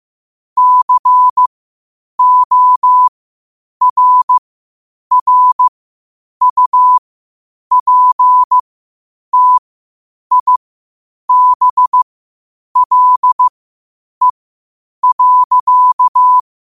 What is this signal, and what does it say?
Morse code, heard as a keyed tone, about 15 wpm, spaced as for 8 wpm, 1000 Hz. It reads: CORRUPTIBLE.
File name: qotdfiles / 8wpm